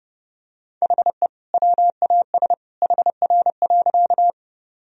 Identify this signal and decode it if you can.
Morse code, heard as a keyed tone, about 32 wpm, 700 Hz. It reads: HE WAS HR.
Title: Sentences from Top 200 Words and Abbreviations 30wpm